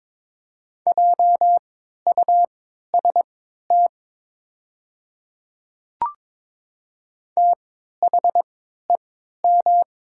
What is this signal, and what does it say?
Morse code, heard as a keyed tone, about 22 wpm, spaced as for 13 wpm, 700 Hz. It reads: JUST THEM
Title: 3x Character Spacing - Top 200 Words - 22wpm